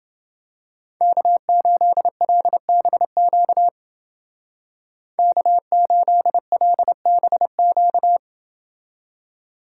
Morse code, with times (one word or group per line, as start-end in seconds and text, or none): 1.0-3.7 K8LBQ
5.2-8.2 K8LBQ